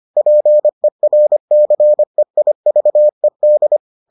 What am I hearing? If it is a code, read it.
Morse code, heard as a keyed tone, about 25 wpm, 600 Hz. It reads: PERCEIVED